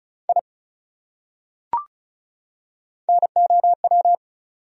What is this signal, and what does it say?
Morse code, heard as a keyed tone, about 36 wpm, 700 Hz. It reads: I NOW
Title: Top 100 Words 35wpm